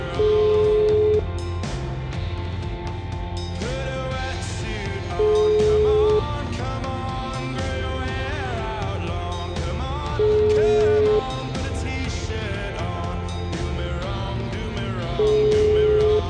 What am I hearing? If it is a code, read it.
Ringback tone (Europe, 425 Hz), with noise.